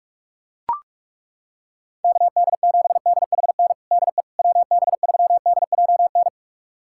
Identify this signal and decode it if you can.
Morse code, heard as a keyed tone, about 42 wpm, 700 Hz. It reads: KD7DHN DE WB3DJN
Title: USA Callsign DE Callsign 45wpm